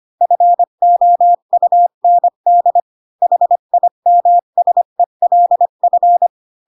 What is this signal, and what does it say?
Morse code, heard as a keyed tone, about 25 wpm, 700 Hz. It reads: FOUND HIMSELF
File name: Pinocchio / Pinocchio ch1 20 25 ik1hge